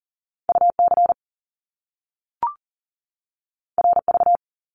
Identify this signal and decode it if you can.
Morse code, heard as a keyed tone, about 40 wpm, 700 Hz. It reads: UC RV
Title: Top Two Letter Combinations in Words 40wpm